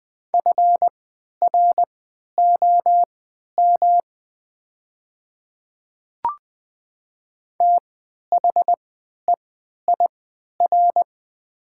Morse code, heard as a keyed tone, about 20 wpm, spaced as for 11 wpm, 700 Hz. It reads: FROM THEIR
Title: 3x Character Spacing - Top 100 Words - 20wpm